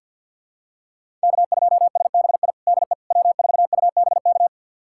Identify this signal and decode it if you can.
Morse code, heard as a keyed tone, about 53 wpm, 700 Hz. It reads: K2SBI DE W4UBK
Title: USA Callsign DE Callsign - 5 Characters - 50wpm